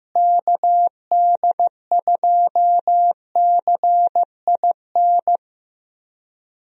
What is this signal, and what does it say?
Morse code, heard as a keyed tone, about 16 wpm, 700 Hz. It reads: KD2CIN
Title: USA Call Signs - 6 Characters - Repeated 3x - 15wpm